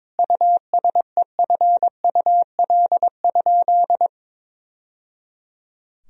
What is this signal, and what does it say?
Morse code, heard as a keyed tone, about 22 wpm, 700 Hz. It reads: USEFUL?
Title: Sentences from Top 1000 Words - Review Part 4 - 22wpm